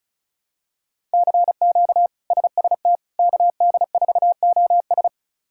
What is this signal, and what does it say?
Morse code, heard as a keyed tone, about 36 wpm, 700 Hz. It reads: CQ SST KD4OS
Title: K1USN SST QSO Practice - 35wpm